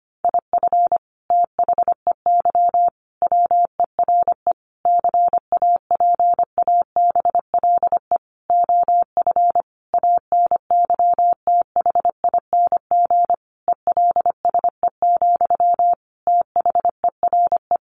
Morse code, heard as a keyed tone, about 25 wpm, 700 Hz. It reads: IF THEY WERE CAPABLE OF ANYTHING ELSE, THERE
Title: Sentences from Top 1200 Words 25wpm